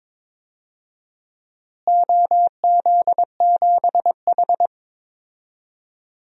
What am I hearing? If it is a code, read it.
Morse code, heard as a keyed tone, about 22 wpm, 700 Hz. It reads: OZ7H
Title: DXCC-100 Call Signs - 4 Characters - Repeated 2x - 22wpm